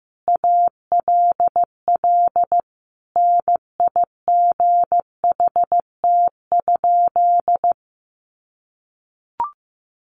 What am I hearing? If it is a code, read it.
Morse code, heard as a keyed tone, about 15 wpm, 700 Hz. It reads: ALL NIGHT?
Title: Sentences from Top 700 Words 15wpm